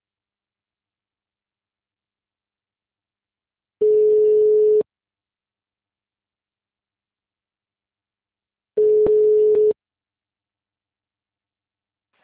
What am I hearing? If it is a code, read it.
Ringback tone (Europe, 425 Hz).